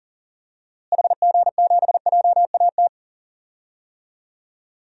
Morse code, heard as a keyed tone, about 40 wpm, 700 Hz. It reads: HG7JAT
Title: DXCC-100 Call Signs - 6 Characters - Repeated 2x - 40wpm